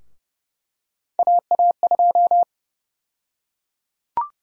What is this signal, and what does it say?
Morse code, heard as a keyed tone, about 30 wpm, 700 Hz. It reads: AA2
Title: USA Call Sign Prefixes - 30wpm